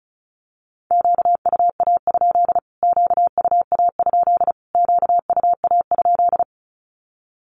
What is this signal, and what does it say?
Morse code, heard as a keyed tone, about 34 wpm, 700 Hz. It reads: QUA? QUA? QUA?
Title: Q-Codes Repeated 3x 35wpm